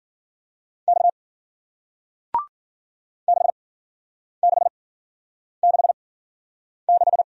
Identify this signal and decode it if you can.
Morse code, heard as a keyed tone, about 53 wpm, spaced as for 9 wpm, 700 Hz. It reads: X 6666